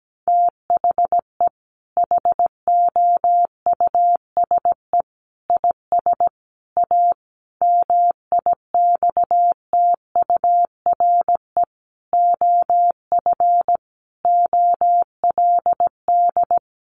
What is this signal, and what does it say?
Morse code, heard as a keyed tone, about 17 wpm, 700 Hz. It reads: THE HOUSE IS A MIXTURE OF OLD
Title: Sentences from Top 1500 Words 17wpm